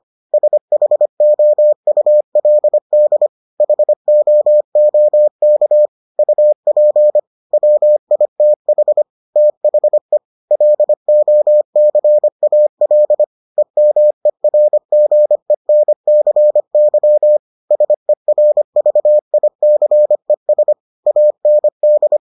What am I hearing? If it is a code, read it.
Morse code, heard as a keyed tone, about 25 wpm, 600 Hz. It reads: SHOULD HOOK UP WITH THE LOCAL EMERGENCY SERVICES AND